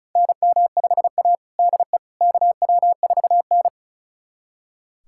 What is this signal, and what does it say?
Morse code, heard as a keyed tone, about 36 wpm, 700 Hz. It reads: NM5A DE KW4N
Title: USA Callsign DE Callsign - 4 Characters - 35wpm